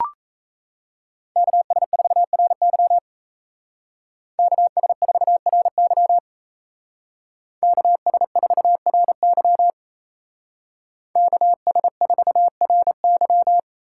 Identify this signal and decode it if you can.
Morse code, heard as a keyed tone, about 36 wpm, 700 Hz. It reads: KS4RY KS4RY KS4RY KS4RY